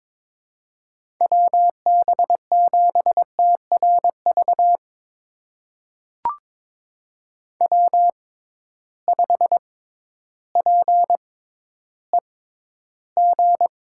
Morse code, heard as a keyed tone, about 22 wpm, spaced as for 12 wpm, 700 Hz. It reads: WB7TRV W5PEG
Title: USA Call Signs - 6x Character Spacing - 22wpm